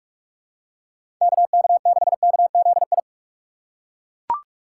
Morse code, heard as a keyed tone, about 41 wpm, 700 Hz. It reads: KK6KZI
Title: USA Call Signs - 6 Characters - 2x Character Spacing - 45wpm